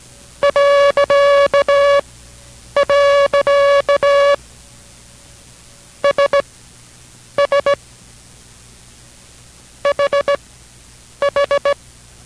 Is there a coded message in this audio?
Yes: Morse code from 0.4 to 11.7 s, heard as a keyed tone, about 17 wpm, spaced as for 8 wpm, 570 Hz. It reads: .. SS HH